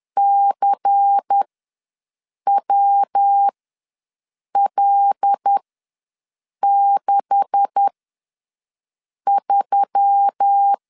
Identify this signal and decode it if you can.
Morse code, heard as a keyed tone, about 10 wpm, spaced as for 6 wpm, 800 Hz. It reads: CWL63
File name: LEZIONI / 9 286 ws 10wpm 4032